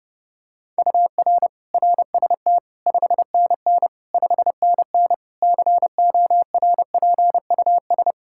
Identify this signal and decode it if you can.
Morse code, heard as a keyed tone, about 30 wpm, 700 Hz. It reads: UR RST 5NN 5NN CORPUS
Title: K1USN SST QSO Practice 30wpm